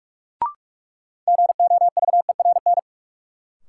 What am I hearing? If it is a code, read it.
Morse code, heard as a keyed tone, about 46 wpm, 700 Hz. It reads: GOVERN
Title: Sentences from Top 1000 Words - Review Part 2 - New Word Drill - 45wpm